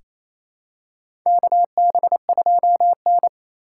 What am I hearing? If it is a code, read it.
Morse code, heard as a keyed tone, about 28 wpm, 700 Hz. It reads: KB2N
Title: USA Call Signs - 4 Characters - 4x Character Spacing - 28wpm